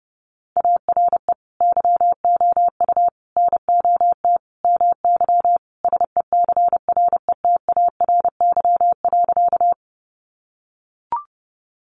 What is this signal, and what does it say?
Morse code, heard as a keyed tone, about 30 wpm, 700 Hz. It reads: ARE YOU NOT MY SECRETARY.
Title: Sentences from Top 1300 Words 30wpm